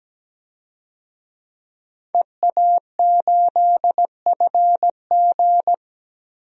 Morse code, heard as a keyed tone, about 17 wpm, 700 Hz. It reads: EA8FG
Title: DXCC-100 Call Signs - 5 Characters - 17wpm